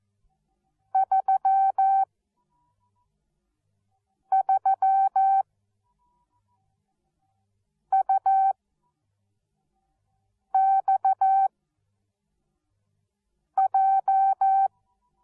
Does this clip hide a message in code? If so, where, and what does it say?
Yes: Morse code from 0.9 to 14.7 s, heard as a keyed tone, about 14 wpm, spaced as for 4 wpm, 770 Hz. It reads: 33UXJ